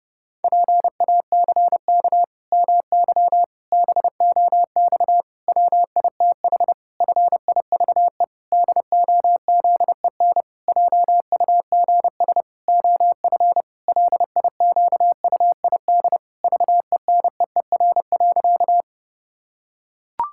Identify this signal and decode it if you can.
Morse code, heard as a keyed tone, about 30 wpm, 700 Hz. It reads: PACK MY BOX WITH FIVE DOZEN JUGS OF LIQUID VENEER.
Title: Pangram Challenge v2 - 30wpm